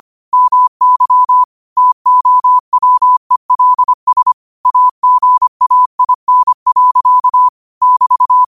Morse code, heard as a keyed tone, about 25 wpm, 1000 Hz. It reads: MY TOWELS AGAIN. <BT>
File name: qotdfiles / 25wpm